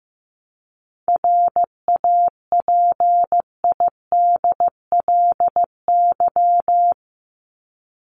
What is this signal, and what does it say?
Morse code, heard as a keyed tone, about 15 wpm, 700 Hz. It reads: RAPIDLY
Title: LY Suffix - Top 100 Words with Base Words 15wpm